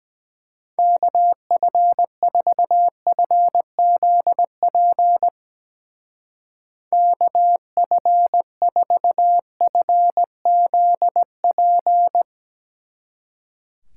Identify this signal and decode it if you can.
Morse code, heard as a keyed tone, about 18 wpm, 700 Hz. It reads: KF4FZP KF4FZP
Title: Speed Racing - USA 2x3 Call Signs 17wpm